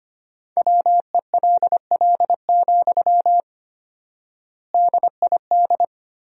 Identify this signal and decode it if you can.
Morse code, heard as a keyed tone, about 25 wpm, 700 Hz. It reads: WELL, DID
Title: 4x Word-Spacing - Sentences from Top 200 Words - 25wpm